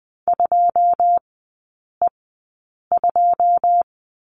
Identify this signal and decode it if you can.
Morse code, heard as a keyed tone, about 20 wpm, spaced as for 9 wpm, 690 Hz. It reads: 2E2
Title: 081 Lesson 14e - Number 2 - Three Characters 20wpm